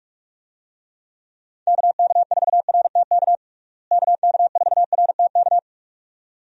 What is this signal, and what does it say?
Morse code, heard as a keyed tone, about 46 wpm, 700 Hz. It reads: KK4RTK KK4RTK
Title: USA Call Signs - 6 Characters - Repeated 2x - 45wpm